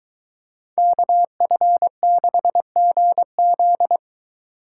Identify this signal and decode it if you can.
Morse code, heard as a keyed tone, about 22 wpm, 700 Hz. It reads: KF6GZ